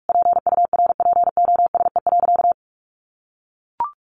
Morse code, heard as a keyed tone, about 44 wpm, 700 Hz. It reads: PURPOSE.
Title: Sentences from Top 1500 Words 45wpm